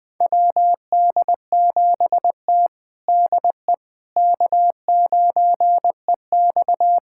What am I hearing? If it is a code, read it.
Morse code, heard as a keyed tone, about 20 wpm, 700 Hz. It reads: WD7T DE K9EX
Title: USA Callsign DE Callsign - 4 Characters - 20wpm